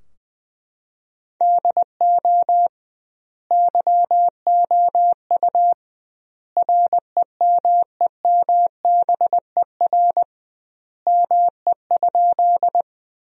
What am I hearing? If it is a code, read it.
Morse code, heard as a keyed tone, about 21 wpm, 700 Hz. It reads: DO YOU REMEMBER ME?
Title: 2x Word-Spacing - Sentences from Top 400 Words - 20wpm